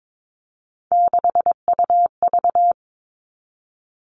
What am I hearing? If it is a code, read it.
Morse code, heard as a keyed tone, about 22 wpm, 700 Hz. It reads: 6UV
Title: All Cut Numbers - 3-Digit 22wpm